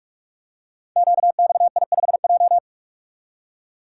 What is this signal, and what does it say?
Morse code, heard as a keyed tone, about 44 wpm, 700 Hz. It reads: QXI5J
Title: Code Groups - 5 Letter:Numbers - 45wpm